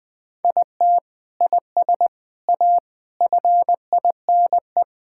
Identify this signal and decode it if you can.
Morse code, heard as a keyed tone, about 20 wpm, 700 Hz. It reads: IT IS A FINE
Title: Sentences from Top 1600 Words 20wpm